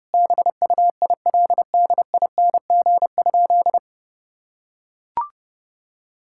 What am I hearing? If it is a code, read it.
Morse code, heard as a keyed tone, about 32 wpm, 700 Hz. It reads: BUILDING?
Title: Sentences from Top 1000 Words - Review Part 5 - 30wpm